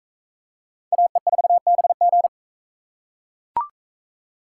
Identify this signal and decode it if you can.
Morse code, heard as a keyed tone, about 40 wpm, 700 Hz. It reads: AE4BG